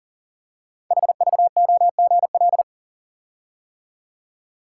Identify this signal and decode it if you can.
Morse code, heard as a keyed tone, about 39 wpm, 700 Hz. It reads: HVOGL
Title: Code Groups - 5 Letter:Numbers - 40wpm